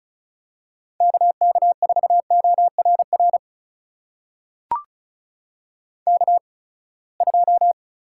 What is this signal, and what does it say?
Morse code, heard as a keyed tone, about 36 wpm, spaced as for 24 wpm, 700 Hz. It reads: KK4ORR K2
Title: USA Call Signs - 8x Character Spacing - 35wpm